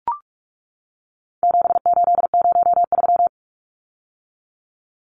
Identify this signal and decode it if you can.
Morse code, heard as a keyed tone, about 44 wpm, 700 Hz. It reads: 7803
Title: Numbers - 4-Digit 45wpm